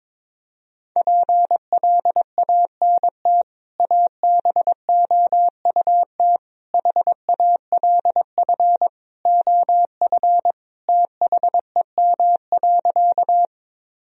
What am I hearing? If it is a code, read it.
Morse code, heard as a keyed tone, about 22 wpm, 700 Hz. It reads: PLANT ABOUT HALF OF THEM.
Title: Sentences from Top 1000 Words - Review Part 1 - 22wpm